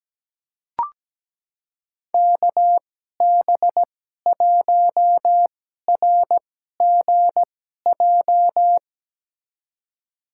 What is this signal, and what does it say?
Morse code, heard as a keyed tone, about 17 wpm, spaced as for 12 wpm, 700 Hz. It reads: KB1RGJ
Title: USA Call Signs - 6 Characters - 2x Character Spacing - 17wpm